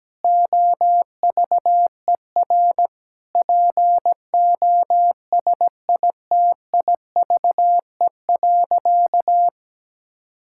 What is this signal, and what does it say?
Morse code, heard as a keyed tone, about 17 wpm, 700 Hz. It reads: OVER POSITIVE.
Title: Sentences from Top 1000 Words - Review Part 5 - 17wpm